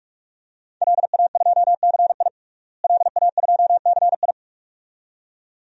Morse code, heard as a keyed tone, about 46 wpm, 700 Hz. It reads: LA2CI LA2CI
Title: DXCC-100 Call Signs - 5 Characters - Repeated 2x - 45wpm